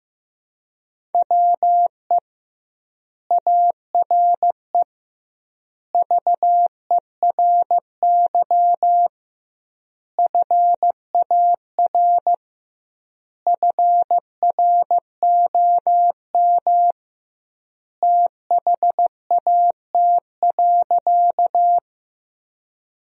Morse code, heard as a keyed tone, about 15 wpm, 700 Hz. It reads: WE ARE VERY FAR FROM THAT.
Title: 2x Word-Spacing - Sentences from Top 300 Words - 15wpm